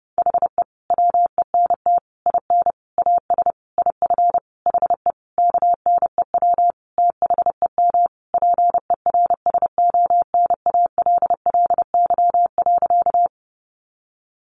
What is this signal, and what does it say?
Morse code, heard as a keyed tone, about 30 wpm, 700 Hz. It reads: HE WENT IN AS IF HE KNEW THEM PERSONALLY.
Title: Sentences from Top 1400 Words 30wpm